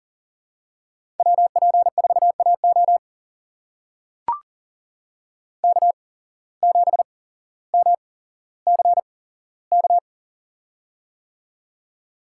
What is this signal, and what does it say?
Morse code, heard as a keyed tone, about 40 wpm, spaced as for 17 wpm, 700 Hz. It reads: WP4AO K7MCK
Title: USA Call Signs - 5 Characters - 8x Character Spacing - 40wpm